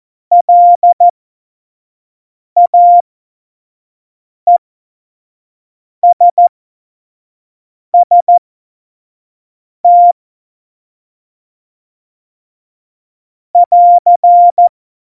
Morse code, heard as a keyed tone, about 14 wpm, spaced as for 5 wpm, 700 Hz. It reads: LAESST <AR>
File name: Morsekurs / Witze / Witz 26 Mai 2020, 25 F70